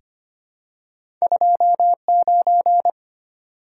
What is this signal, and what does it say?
Morse code, heard as a keyed tone, about 25 wpm, 700 Hz. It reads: 29
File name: Two Letter-Number Combinations - 25wpm